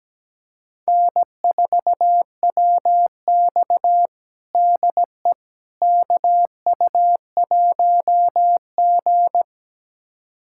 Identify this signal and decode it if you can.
Morse code, heard as a keyed tone, about 17 wpm, 700 Hz. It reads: N4WX DE KU1G
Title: USA Callsign DE Callsign - 4 Characters - 17wpm